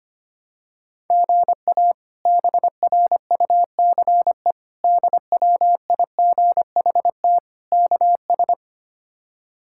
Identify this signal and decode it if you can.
Morse code, heard as a keyed tone, about 25 wpm, 700 Hz. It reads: GA BRUCE DWIGHT KS
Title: K1USN SST QSO Practice - 25wpm